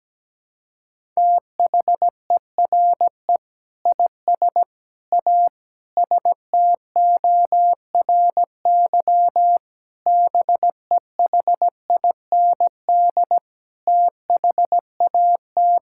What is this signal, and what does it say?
Morse code, heard as a keyed tone, about 17 wpm, 700 Hz. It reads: THERE IS A STORY BEHIND THAT
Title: Sentences from Top 1000 Words - Review Part 5 - 17wpm